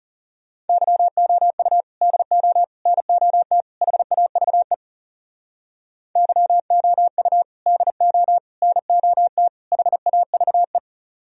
Morse code, heard as a keyed tone, about 40 wpm, 700 Hz. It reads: YOU DO NOT HAVE YOU DO NOT HAVE